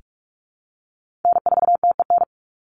Morse code, heard as a keyed tone, about 44 wpm, 700 Hz. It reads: N4TEN
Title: USA Call Signs 45wpm